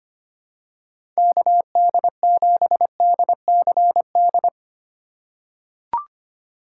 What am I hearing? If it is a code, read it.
Morse code, heard as a keyed tone, about 25 wpm, 700 Hz. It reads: KD7DCD
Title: USA Call Signs - 2x Character Spacing - 25wpm